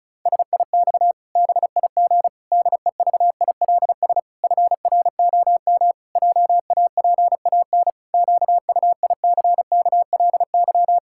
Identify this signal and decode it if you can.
Morse code, heard as a keyed tone, about 34 wpm, 700 Hz. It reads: SIX BIG DEVILS FROM JAPAN QUICKLY
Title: Pangram Challenge v2 - 35wpm